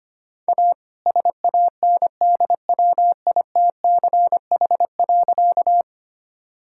Morse code, heard as a keyed tone, about 25 wpm, 700 Hz. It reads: A SANDWITCH.
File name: practice / 25wpm / All-Age Appropriate Jokes - 25wpm